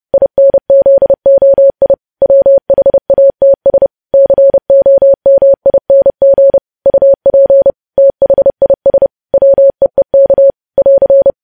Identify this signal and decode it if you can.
Morse code, heard as a keyed tone, about 30 wpm, 550 Hz. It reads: INZOI WHATS COMING UP THIS WEEK <AR>
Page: news headlines in morse code at 30 wpm